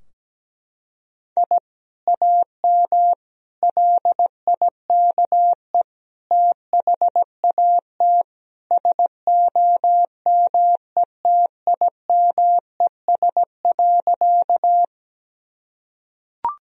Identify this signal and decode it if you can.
Morse code, heard as a keyed tone, about 17 wpm, 700 Hz. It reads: I AM LIKE THAT SOMETIMES.
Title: Sentences from Top 1000 Words - Review Part 3 - 17wpm